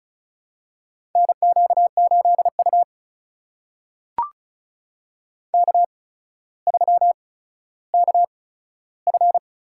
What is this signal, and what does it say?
Morse code, heard as a keyed tone, about 36 wpm, spaced as for 15 wpm, 700 Hz. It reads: NQ8U K3KF
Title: USA Call Signs - 4 Characters - 8x Character Spacing - 35wpm